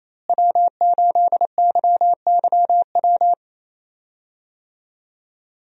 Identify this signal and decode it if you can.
Morse code, heard as a keyed tone, about 28 wpm, 700 Hz. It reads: W8YYW
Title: USA Call Signs - 5 Characters - Repeated 3x - 28wpm